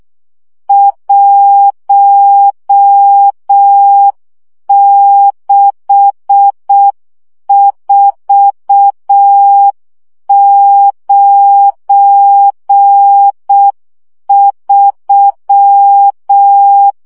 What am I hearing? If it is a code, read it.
Morse code, heard as a keyed tone, about 6 wpm, 800 Hz. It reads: 16493
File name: morze vizsga 2011-2 5